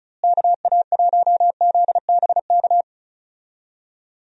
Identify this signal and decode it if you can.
Morse code, heard as a keyed tone, about 34 wpm, 700 Hz. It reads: KA1ZBK
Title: USA Call Signs - 6 Characters - Repeated 3x - 35wpm